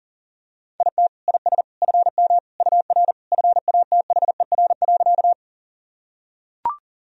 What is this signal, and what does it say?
Morse code, heard as a keyed tone, about 37 wpm, 700 Hz. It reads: IT IS FM UR FATHER.
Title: Sentences from Top 300 Words and Abbreviations 40wpm